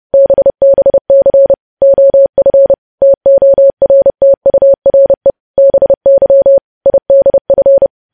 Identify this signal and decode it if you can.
Morse code, heard as a keyed tone, about 30 wpm, 550 Hz. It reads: BBC OF TORTURE BY IDF
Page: news headlines in morse code at 30 wpm